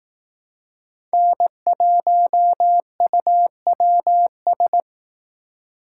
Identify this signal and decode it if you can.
Morse code, heard as a keyed tone, about 18 wpm, 700 Hz. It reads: N1UWS
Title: Speed Racing - USA 1x3 Call Signs 15wpm